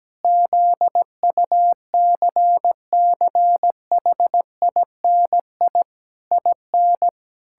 Morse code, heard as a keyed tone, about 17 wpm, 700 Hz. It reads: ZUCCHINI IN
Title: Pangram Challenge v2 - 17wpm